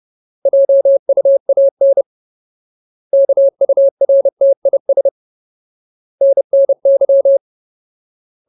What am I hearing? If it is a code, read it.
Morse code, heard as a keyed tone, about 30 wpm, 550 Hz. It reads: JUAN KURTIS NNY
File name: Ultimate Word List / 30WPM Ultimate